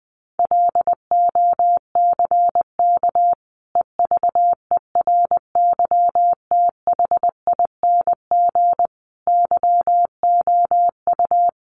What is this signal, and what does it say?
Morse code, heard as a keyed tone, about 20 wpm, 700 Hz. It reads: LOCK EVERYTHING YOU
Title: Sentences from Top 1300 Words 20wpm